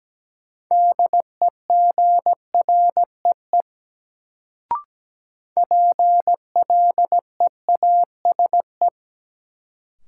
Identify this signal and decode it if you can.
Morse code, heard as a keyed tone, about 17 wpm, 700 Hz. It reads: DEGREE PLEASE
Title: Sentences from Top 1500 Words - New Word Drill Alt - 17wpm